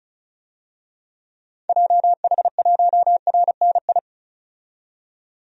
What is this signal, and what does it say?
Morse code, heard as a keyed tone, about 34 wpm, 700 Hz. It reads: JH1RNI
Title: DXCC-100 Call Signs - 6 Characters - 35wpm